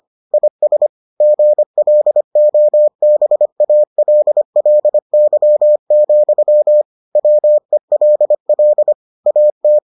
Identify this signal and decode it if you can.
Morse code, heard as a keyed tone, about 25 wpm, 600 Hz. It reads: IS GLOBALLY, WELL AT